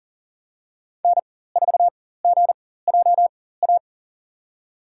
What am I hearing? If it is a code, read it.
Morse code, heard as a keyed tone, about 40 wpm, spaced as for 19 wpm, 700 Hz. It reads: N4GJA
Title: USA Call Signs - 5 Characters - 4x Character Spacing - 40wpm